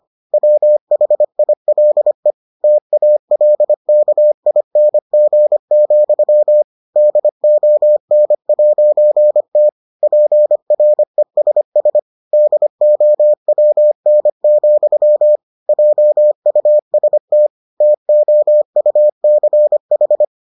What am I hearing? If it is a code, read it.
Morse code, heard as a keyed tone, about 25 wpm, 600 Hz. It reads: WHILE TALKING, DON'T PRESS DOWN, JUST TOUCH